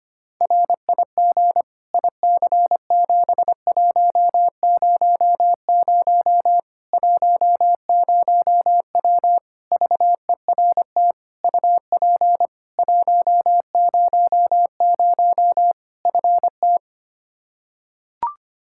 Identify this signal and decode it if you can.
Morse code, heard as a keyed tone, about 25 wpm, 700 Hz. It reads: RIG IC7100 10W VERT UP 100 FT
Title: Practice QSO - Exchange 2 - Info - 25wpm